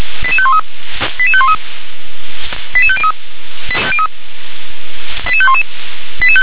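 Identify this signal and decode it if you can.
Five-tone selective call (ZVEI), heard as quick runs of five tones, tones of about 70 ms. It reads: 88513 88513 88503 88511
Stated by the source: Five-tone ZVEI selective call received with an SDR.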